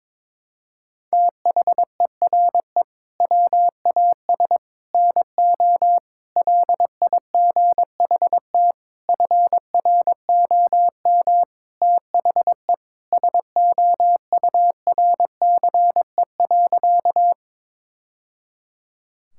Morse code, heard as a keyed tone, about 22 wpm, 700 Hz. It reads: THERE WAS NO LIGHT FROM THE SOURCE.
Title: Sentences from Top 1000 Words - Review Part 5 - 22wpm